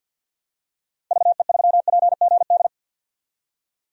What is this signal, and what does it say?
Morse code, heard as a keyed tone, about 52 wpm, 700 Hz. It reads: VE3PGD
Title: DXCC-100 Call Signs - 6 Characters - 50wpm